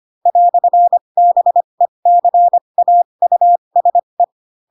Morse code, heard as a keyed tone, about 25 wpm, spaced as for 22 wpm, 700 Hz. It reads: "BECAUSE
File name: Pinocchio / Pinocchio ch7 18 25 ik1hge